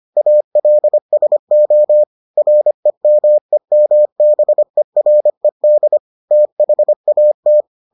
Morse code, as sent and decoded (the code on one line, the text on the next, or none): .- .-.. ... --- / .-. -- . -- -... .-. -.. / - .... .- -
ALSO REMEMBERED THAT